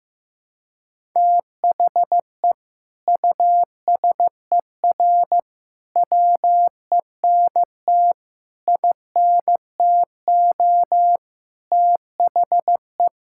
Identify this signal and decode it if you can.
Morse code, heard as a keyed tone, about 16 wpm, 700 Hz. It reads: THE USER WENT INTO THE
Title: Sentences from Top 1000 Words - Review Part 5 - 15wpm